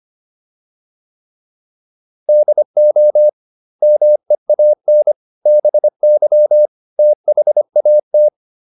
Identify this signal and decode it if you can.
Morse code, heard as a keyed tone, about 25 wpm, 600 Hz. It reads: DO MEAN BY THAT